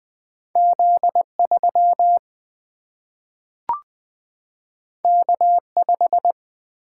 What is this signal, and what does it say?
Morse code, heard as a keyed tone, about 20 wpm, 700 Hz. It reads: Z 3 K 5
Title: Single-Letter Single-Number 20wpm